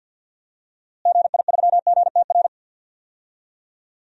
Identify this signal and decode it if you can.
Morse code, heard as a keyed tone, about 51 wpm, 700 Hz. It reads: GI3GTR
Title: DXCC-100 Call Signs - 6 Characters - Repeated 2x - 50wpm